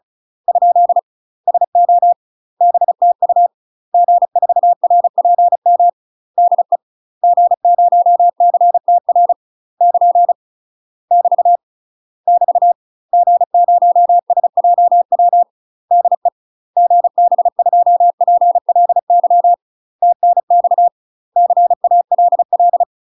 Morse code, heard as a keyed tone, about 35 wpm, 700 Hz. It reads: ? SO BTU G4RPM DE G0CTR <KN> <BT> <BT> G0SJW DE GB2PLY TNX CALL
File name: UK MASTER QSOs TRAINING / 35WPM / UK MASTER QSOs 0018